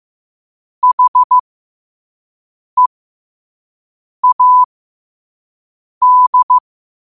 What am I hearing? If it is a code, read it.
Morse code, heard as a keyed tone, about 15 wpm, spaced as for 5 wpm, 1000 Hz. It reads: HEAD
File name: qotdfiles / 5wpm